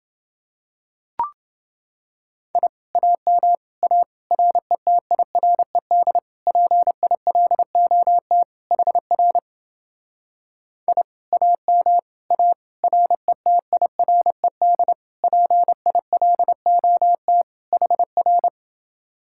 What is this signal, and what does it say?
Morse code, heard as a keyed tone, about 29 wpm, 700 Hz. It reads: I AM A RETIRED PILOT HR I AM A RETIRED PILOT HR